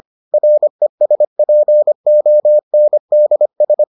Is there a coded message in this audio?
Yes: Morse code from 0.3 to 3.8 s, heard as a keyed tone, about 25 wpm, 600 Hz. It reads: RESPONDS